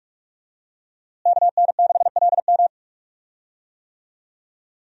Morse code, heard as a keyed tone, about 42 wpm, 700 Hz. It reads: KN6LM
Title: USA Call Signs - 5 Characters - Repeated 3x - 45wpm